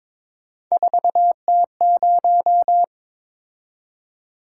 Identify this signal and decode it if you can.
Morse code, heard as a keyed tone, about 22 wpm, 700 Hz. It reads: 4T0
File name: practice / 22wpm / Three Letter-Number Combinations - 22wpm